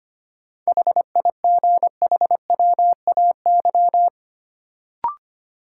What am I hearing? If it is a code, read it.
Morse code, heard as a keyed tone, about 25 wpm, 700 Hz. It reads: HIGHWAY
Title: Sentences from Top 1400 Words - New Word Drill - 25wpm